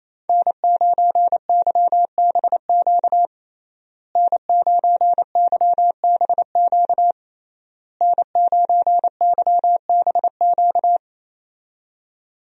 Morse code, heard as a keyed tone, about 28 wpm, 700 Hz. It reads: N9YBQ N9YBQ N9YBQ
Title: USA Call Signs - 5 Characters - Repeated 3x - 28wpm